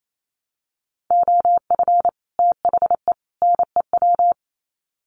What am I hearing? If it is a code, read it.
Morse code, heard as a keyed tone, about 28 wpm, 700 Hz. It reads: OF THE NEW
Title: Sets of 3 Words 28wpm